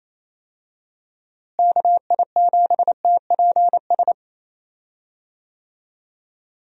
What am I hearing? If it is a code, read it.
Morse code, heard as a keyed tone, about 28 wpm, 700 Hz. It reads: KI7TPS
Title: USA Call Signs - 6 Characters - Repeated 2x - 28wpm